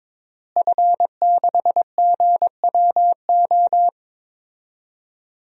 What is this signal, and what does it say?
Morse code, heard as a keyed tone, about 22 wpm, 700 Hz. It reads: F6GWO
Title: DXCC-100 Call Signs - 5 Characters - Repeated 3x - 22wpm